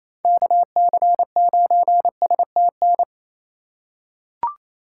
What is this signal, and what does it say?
Morse code, heard as a keyed tone, about 27 wpm, 700 Hz. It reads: KC9STN
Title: USA Call Signs - 3x Character Spacing - 28wpm